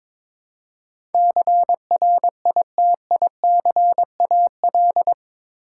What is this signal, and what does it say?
Morse code, heard as a keyed tone, about 21 wpm, 700 Hz. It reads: CRITICAL
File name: practice / 22wpm / Eight Letter Words - 22wpm